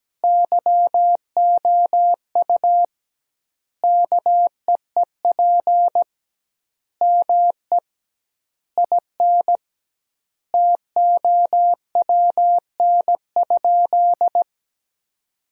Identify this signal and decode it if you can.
Morse code, heard as a keyed tone, about 17 wpm, 700 Hz. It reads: YOU KEEP ME IN TOWN?
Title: 2x Word-Spacing - Sentences from Top 500 Words - 17wpm